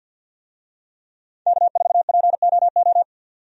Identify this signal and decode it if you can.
Morse code, heard as a keyed tone, about 51 wpm, 700 Hz. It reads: K4POO
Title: USA Call Signs - 5 Characters - 50wpm